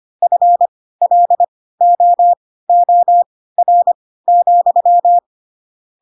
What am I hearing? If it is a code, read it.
Morse code, heard as a keyed tone, about 25 wpm, spaced as for 16 wpm, 700 Hz. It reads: FLOOR,
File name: Pinocchio / Pinocchio ch4 10 25 ik1hge